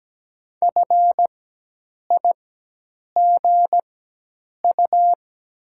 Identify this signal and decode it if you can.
Morse code, heard as a keyed tone, about 17 wpm, spaced as for 8 wpm, 700 Hz. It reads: FIGU
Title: 4x Character Spacing - Top 500 Words - 17wpm